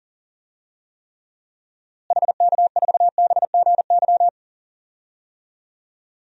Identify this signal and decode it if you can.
Morse code, heard as a keyed tone, about 37 wpm, 700 Hz. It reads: HK4BGY
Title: DXCC-100 Call Signs - 6 Characters - Repeated 3x - 40wpm